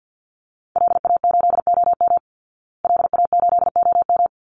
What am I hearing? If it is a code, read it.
Morse code, heard as a keyed tone, about 49 wpm, 700 Hz. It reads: LA8OM LA8OM
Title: Call Signs Repeated 3x - 50wpm